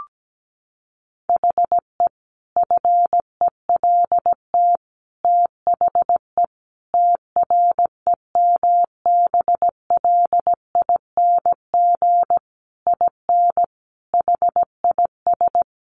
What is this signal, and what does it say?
Morse code, heard as a keyed tone, about 17 wpm, 700 Hz. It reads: HE FELT THE TREMBLING IN HIS
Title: A Sound Of Thunder 17wpm